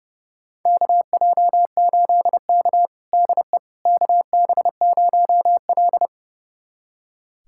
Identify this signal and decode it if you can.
Morse code, heard as a keyed tone, about 30 wpm, 700 Hz. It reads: KJ8K DE KB0L
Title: USA Callsign DE Callsign - 4 Characters - 30wpm